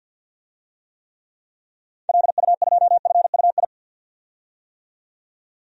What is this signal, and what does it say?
Morse code, heard as a keyed tone, about 53 wpm, 700 Hz. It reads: LU2FUI
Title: DXCC-100 Call Signs - 6 Characters - 50wpm